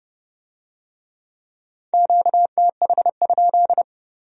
Morse code, heard as a keyed tone, about 29 wpm, 700 Hz. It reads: QTH?